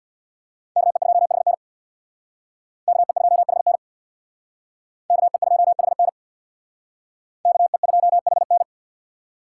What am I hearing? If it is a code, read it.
Morse code, heard as a keyed tone, about 62 wpm, 700 Hz. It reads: KE2HN KE2HN KE2HN KE2HN
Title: Speed Racing - USA Call Signs 50wpm